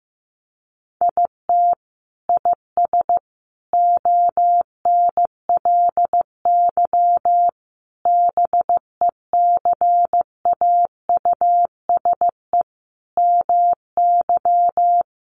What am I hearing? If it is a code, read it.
Morse code, heard as a keyed tone, about 14 wpm, 700 Hz. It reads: IT IS ONLY BECAUSE MY
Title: Sentences from Top 1400 Words 15wpm